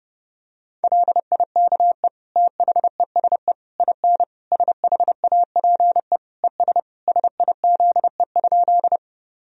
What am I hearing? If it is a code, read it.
Morse code, heard as a keyed tone, about 30 wpm, 700 Hz. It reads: LIKE THESE IN SHAPE ES SIZE?
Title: Sentences from Top 500 Words and Abbreviations 30wpm